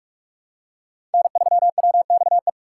Morse code, heard as a keyed tone, about 46 wpm, 700 Hz. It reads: N3WXE
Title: USA Call Signs - 5 Characters - 4x Character Spacing - 45wpm